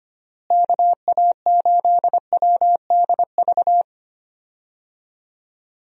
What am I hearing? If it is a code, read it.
Morse code, heard as a keyed tone, about 25 wpm, 700 Hz. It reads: KA8WDV
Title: USA Call Signs - 6 Characters - Repeated 3x - 25wpm